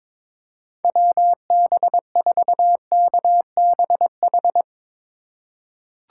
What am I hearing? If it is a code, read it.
Morse code, heard as a keyed tone, about 22 wpm, 700 Hz. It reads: WB4KBH
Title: USA Call Signs - 6 Characters - 3x Character Spacing - 22wpm